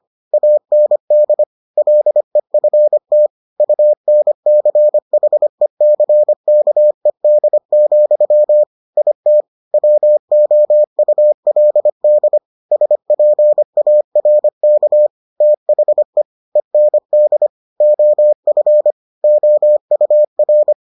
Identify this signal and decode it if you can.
Morse code, heard as a keyed tone, about 25 wpm, 600 Hz. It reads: AND LEFT UNCHECKED, IT WOULD SPARK THE END OF OUR